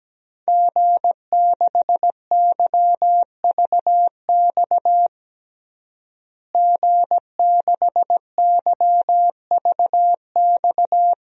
Morse code, heard as a keyed tone, about 17 wpm, 700 Hz. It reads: G6YVX G6YVX
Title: DXCC-100 Call Signs - 5 Characters - Repeated 3x - 17wpm